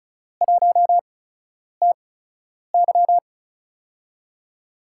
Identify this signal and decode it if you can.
Morse code, heard as a keyed tone, about 36 wpm, spaced as for 10 wpm, 700 Hz. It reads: 1TY
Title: USA Call Signs - 4 Characters - 8x Character Spacing - 35wpm